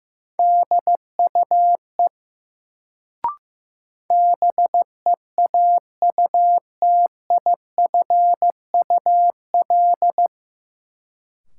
Morse code, heard as a keyed tone, about 15 wpm, 700 Hz. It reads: DUE BEAUTIFUL
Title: Sentences from Top 900 Words - New Word Drill - 15wpm